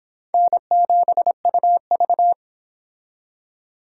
Morse code, heard as a keyed tone, about 26 wpm, 700 Hz. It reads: N7UV